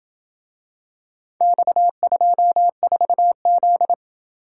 Morse code, heard as a keyed tone, about 26 wpm, 700 Hz. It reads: X24Z